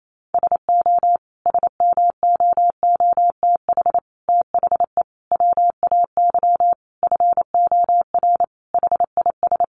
Morse code, heard as a keyed tone, about 28 wpm, 700 Hz. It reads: SO SMOOTH THE WAY FOR HIS
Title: Sentences from Top 1400 Words 28wpm